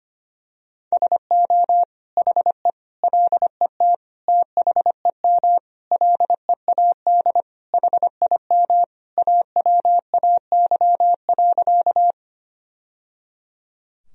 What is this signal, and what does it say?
Morse code, heard as a keyed tone, about 25 wpm, 700 Hz. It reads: SO HE LET THEM LEAD HIM AWAY.
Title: Sentences from Top 1000 Words - Review Part 2 - 25wpm